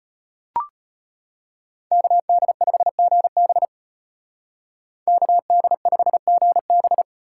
Morse code, heard as a keyed tone, about 33 wpm, 700 Hz. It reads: KD5GB KD5GB